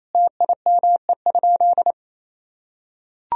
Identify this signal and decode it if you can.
Morse code, heard as a keyed tone, about 28 wpm, 700 Hz. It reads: TIME?
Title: Sentences from Top 1000 Words - Review Part 1 - 28wpm